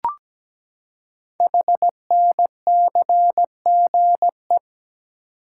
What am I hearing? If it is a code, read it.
Morse code, heard as a keyed tone, about 17 wpm, 700 Hz. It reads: HNCGE
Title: Code Groups - 5 Letters - 17wpm